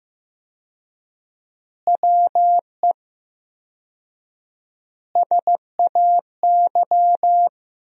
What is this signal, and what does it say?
Morse code, heard as a keyed tone, about 16 wpm, 700 Hz. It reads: WE SAY
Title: 4x Word-Spacing - Sentences from Top 200 Words - 15wpm